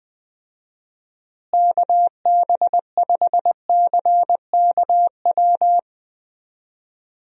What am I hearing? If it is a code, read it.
Morse code, heard as a keyed tone, about 21 wpm, 700 Hz. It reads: KB5CKW
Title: USA Call Signs - 6 Characters - Repeated 2x - 20wpm